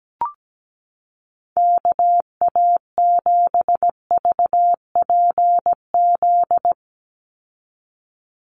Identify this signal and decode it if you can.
Morse code, heard as a keyed tone, about 17 wpm, 700 Hz. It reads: KA7VPZ
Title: USA Call Signs 17wpm